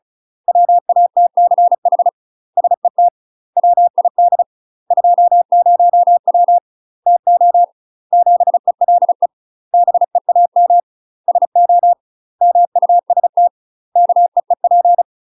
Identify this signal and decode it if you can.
Morse code, heard as a keyed tone, about 36 wpm, 700 Hz. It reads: WATCH SET WID 20W TO 7ELE BEAM SO MUST KEEP
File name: UK MASTER QSOs TRAINING / 35WPM / UK MASTER QSOs 0017